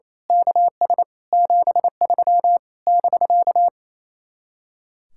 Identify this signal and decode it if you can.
Morse code, heard as a keyed tone, about 26 wpm, 700 Hz. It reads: KS 73 <BK>
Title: Speed-Racing Common QSO Phrases - 28wpm